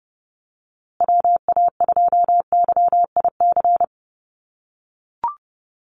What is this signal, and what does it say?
Morse code, heard as a keyed tone, about 30 wpm, 700 Hz. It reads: WA2YIC
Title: USA Call Signs Repeated 3x 30wpm